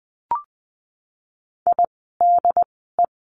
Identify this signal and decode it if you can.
Morse code, heard as a keyed tone, about 20 wpm, spaced as for 15 wpm, 700 Hz. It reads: IDE
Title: 108 Lesson 18f - Letter m - Words 20wpm